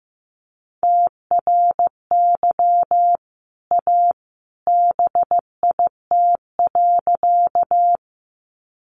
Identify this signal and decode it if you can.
Morse code, heard as a keyed tone, about 15 wpm, 700 Hz. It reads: TRY A BIT.
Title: Sentences from Top 900 Words 15wpm